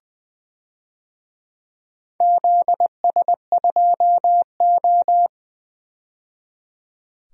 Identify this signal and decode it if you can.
Morse code, heard as a keyed tone, about 20 wpm, 700 Hz. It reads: ZS2O
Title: DXCC-100 Call Signs - 4 Characters - 20wpm